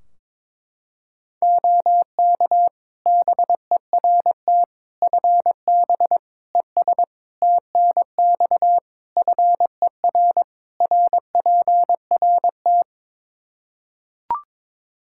Morse code, heard as a keyed tone, about 22 wpm, 700 Hz. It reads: OK BERT FB ES TNX FER RPRT
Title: Practice QSO - Exchange 2 - Info - 22wpm